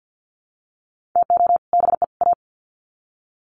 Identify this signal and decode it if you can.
Morse code, heard as a keyed tone, about 50 wpm, 700 Hz. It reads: TO BE A
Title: Sets of 3 Words 50wpm